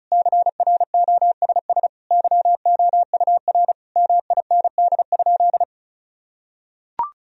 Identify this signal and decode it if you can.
Morse code, heard as a keyed tone, about 34 wpm, 700 Hz. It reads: CROSS YOUR MIND?
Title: Sentences from Top 1000 Words - Review Part 1 - 35wpm